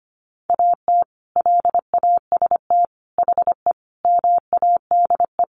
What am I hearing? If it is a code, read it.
Morse code, heard as a keyed tone, about 25 wpm, 700 Hz. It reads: AT LAST HE MADE
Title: Sentences from Top 1200 Words 25wpm